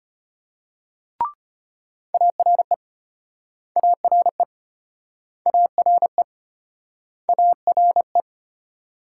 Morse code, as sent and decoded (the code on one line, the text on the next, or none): .- .-. / .- .-. / .- .-. / .- .-. .
ARE ARE ARE ARE